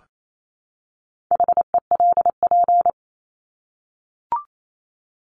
Morse code, heard as a keyed tone, about 28 wpm, 700 Hz. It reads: HELP
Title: Top 200 Words 28wpm